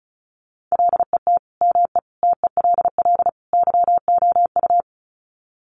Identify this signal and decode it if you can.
Morse code, heard as a keyed tone, about 34 wpm, 700 Hz. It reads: LET ME TELL YOU
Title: Sets of 4 Words 35wpm